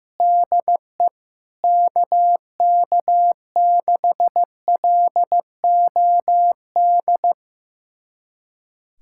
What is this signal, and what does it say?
Morse code, heard as a keyed tone, about 15 wpm, 700 Hz. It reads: DE KK6LOD
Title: USA Callsign DE Callsign - 6 Characters - 15wpm